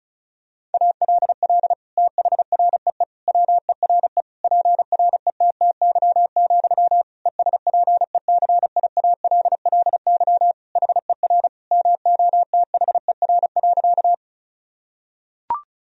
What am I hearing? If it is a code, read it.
Morse code, heard as a keyed tone, about 36 wpm, 700 Hz. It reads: ALL THREE WERE PRETTY, ESPECIALLY HER MOTHER.
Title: Sentences from Top 1000 Words - Review Part 4 - 35wpm